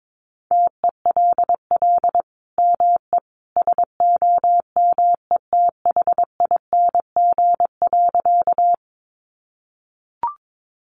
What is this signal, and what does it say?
Morse code, heard as a keyed tone, about 22 wpm, 700 Hz. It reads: TELL ME SOMETHING.
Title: Sentences from Top 600 Words 22wpm